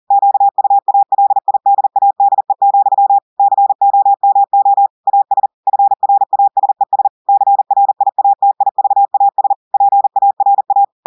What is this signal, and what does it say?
Morse code, heard as a keyed tone, about 40 wpm, 800 Hz. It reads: QUALIDADE, COMO AS FRASES CRIATIVAS PARA
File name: audios / portugais / portugais texte en clair 2 vitesse 40 mots minutes